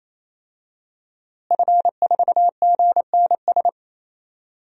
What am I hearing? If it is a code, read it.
Morse code, heard as a keyed tone, about 28 wpm, 700 Hz. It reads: F4GNS
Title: DXCC-100 Call Signs - 5 Characters - Repeated 3x - 28wpm